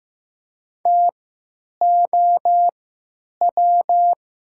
Morse code, heard as a keyed tone, about 15 wpm, spaced as for 9 wpm, 700 Hz. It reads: TOW
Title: 3x Character Spacing - Top 500 Words - 15wpm